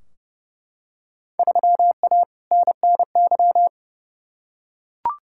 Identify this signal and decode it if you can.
Morse code, heard as a keyed tone, about 30 wpm, 700 Hz. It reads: 3A NNY
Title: Field Day Exchanges - 30wpm